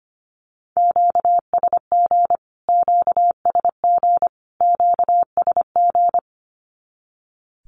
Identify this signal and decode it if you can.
Morse code, heard as a keyed tone, about 25 wpm, 700 Hz. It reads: QSG QSG QSG
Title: Q-Codes Repeated 3x 25wpm